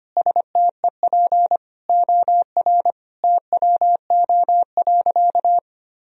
Morse code, heard as a keyed tone, about 25 wpm, 700 Hz. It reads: STEP OR TWO.
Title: Sentences from Top 1000 Words - Review Part 2 - 25wpm